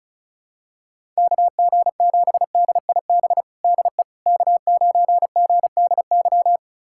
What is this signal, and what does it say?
Morse code, heard as a keyed tone, about 36 wpm, 700 Hz. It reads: KG7DIB DE K9GDY
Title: USA Callsign DE Callsign 35wpm